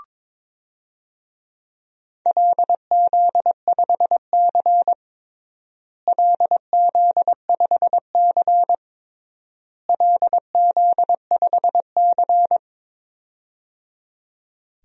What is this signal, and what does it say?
Morse code, heard as a keyed tone, about 22 wpm, 700 Hz. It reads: LZ5C LZ5C LZ5C
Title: DXCC-100 Call Signs - 4 Characters - Repeated 3x - 22wpm